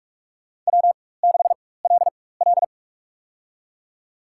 Morse code, heard as a keyed tone, about 46 wpm, spaced as for 21 wpm, 700 Hz. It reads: W6LL
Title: USA Call Signs - 4 Characters - 4x Character Spacing - 45wpm